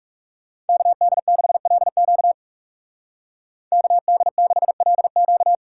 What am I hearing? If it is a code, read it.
Morse code, heard as a keyed tone, about 34 wpm, 700 Hz. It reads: KD6LQ KD6LQ
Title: Speed Racing - USA 2x2 Call Signs 30wpm